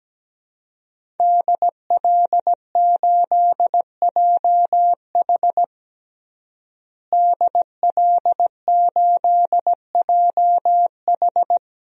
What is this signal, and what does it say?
Morse code, heard as a keyed tone, about 17 wpm, 700 Hz. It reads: DL8JH DL8JH
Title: DXCC-100 Call Signs - 5 Characters - Repeated 3x - 17wpm